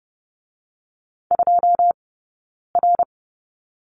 Morse code, heard as a keyed tone, about 30 wpm, 700 Hz. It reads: 2 R
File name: Lesson 19d - Letter w - Two Characters 30wpm